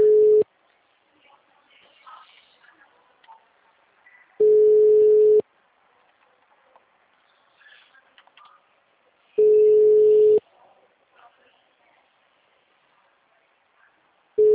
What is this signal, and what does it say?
Ringback tone (Europe, 425 Hz).